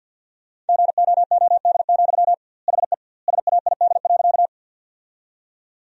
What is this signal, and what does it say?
Morse code, heard as a keyed tone, about 50 wpm, 700 Hz. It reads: GOOD, HE SAID.
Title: 2x Word-Spacing - Sentences from Top 200 Words - 50wpm